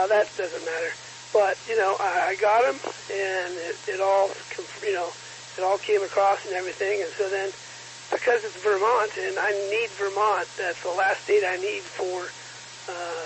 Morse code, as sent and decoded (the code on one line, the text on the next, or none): none